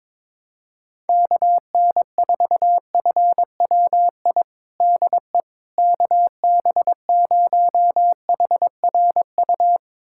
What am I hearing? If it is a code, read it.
Morse code, heard as a keyed tone, about 21 wpm, 700 Hz. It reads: KN4FWI DE KB0HRU
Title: USA Callsign DE Callsign - 6 Characters - 22wpm